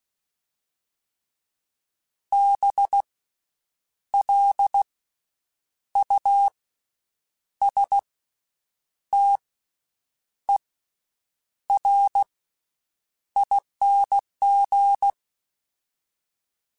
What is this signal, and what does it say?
Morse code, heard as a keyed tone, about 16 wpm, spaced as for 7 wpm, 800 Hz. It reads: BLUSTERING
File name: Special INGs-mp3s / Special INGs-mp3s / Slow ing @15